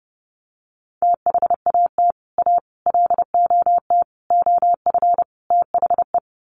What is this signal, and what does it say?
Morse code, heard as a keyed tone, about 30 wpm, 700 Hz. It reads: THAT A LOT OF THE
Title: Sets of 5 Words Encore2 30wpm